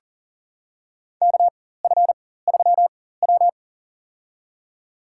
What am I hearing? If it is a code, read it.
Morse code, heard as a keyed tone, about 40 wpm, spaced as for 19 wpm, 700 Hz. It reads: KF3W
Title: USA Call Signs - 4 Characters - 4x Character Spacing - 40wpm